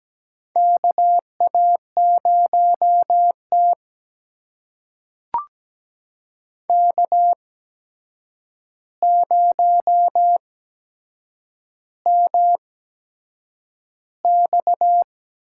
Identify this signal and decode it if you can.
Morse code, heard as a keyed tone, about 17 wpm, spaced as for 7 wpm, 700 Hz. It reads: KA0T K0MX
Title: USA Call Signs - 4 Characters - 8x Character Spacing - 17wpm